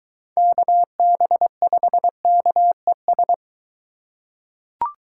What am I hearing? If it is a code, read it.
Morse code, heard as a keyed tone, about 22 wpm, 700 Hz. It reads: KB5KES